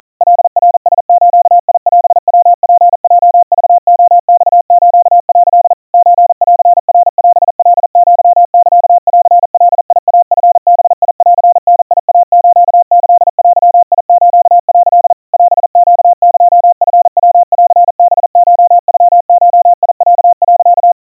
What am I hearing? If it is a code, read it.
Morse code (Japanese Wabun code), heard as a keyed tone, about 40 wpm, 700 Hz. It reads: カナラズオヤツヲクレマス」 ソンナオカアサンガイチバンダイスキデス」 オネエチヤンハコノゴロ、
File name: kawa / Morse sample / kamyu 150 cpm 700 Hz Crispy